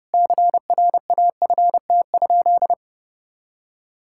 Morse code, heard as a keyed tone, about 30 wpm, 700 Hz. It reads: CRAFT?
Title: Sentences from Top 1000 Words - Review Part 6 - 30wpm